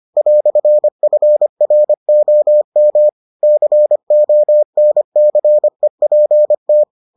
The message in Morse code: "FROM CONCEPT

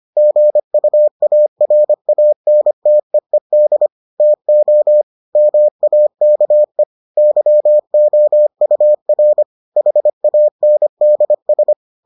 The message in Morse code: GUARANTEED TO MAKE YOUR HANDS

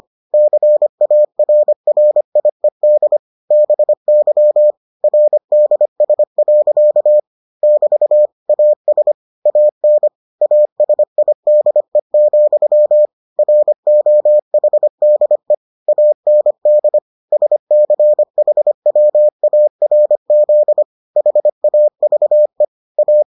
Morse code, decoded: CARRIED BY RDS. <BT> AS AN ASIDE, ROHDE AND SCHWARZ HAVE A

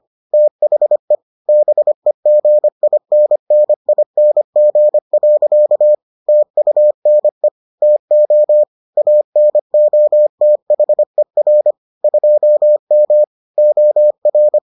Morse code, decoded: THE BEGINNING. TUNE TO ANOTHER 2M OR

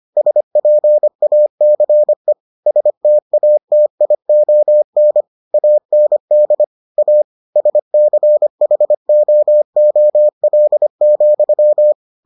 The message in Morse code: SPACE STATION AND A SCHOOL,

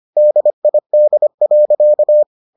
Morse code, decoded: DID.

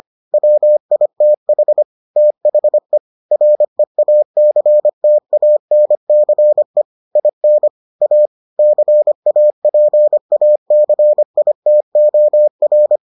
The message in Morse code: WITH THE REACTANCE IN A CAPACITOR

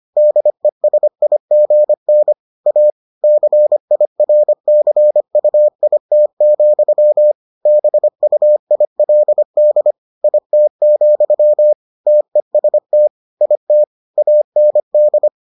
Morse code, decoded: DESIGN A CIRCUIT, BUILD IT, TEST IT AND

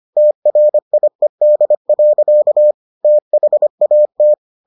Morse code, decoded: TRIED. THAT

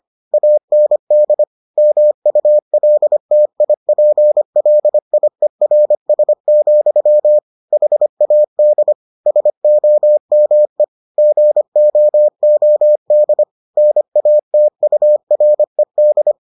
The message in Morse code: AND MULTIPLIERS, HAD SOME GOOD NATURED

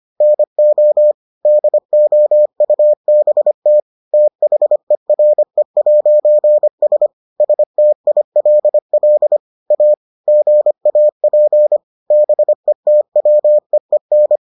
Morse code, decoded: NO DOUBT THERE'S STILL A GAP BETWEEN

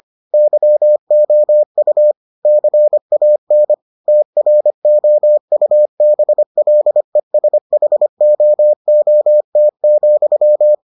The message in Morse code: YOU CAN TROUBLESHOOT,

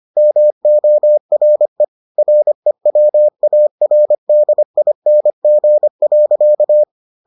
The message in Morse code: MORE REWARDING.

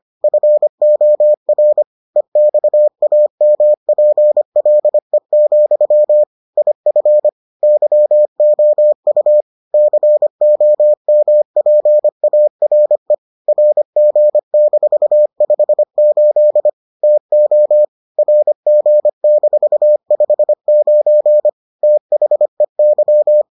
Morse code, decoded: FOR EXAMPLE, IF YOU COMPARE RG-58 TO RG-59 THEY

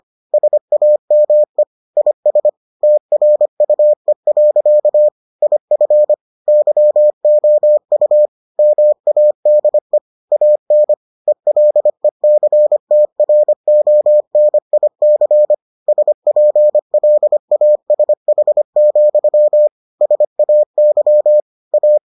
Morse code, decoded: SAME IS TRUE. IF YOU MADE AN ELECTRONIC SPLASH, SAY A

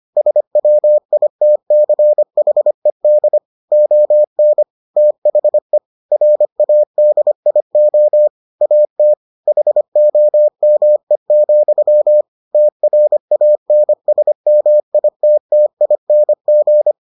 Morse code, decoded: SWITCHED ON THE RADIO AT HOME, TRANSMITTING

